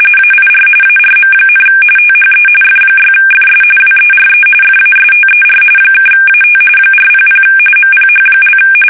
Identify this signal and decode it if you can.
RTTY, about 45.45 baud. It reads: AIR ARM, THE F6FS WERE FASTER (AT 400 MPH)
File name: fsk850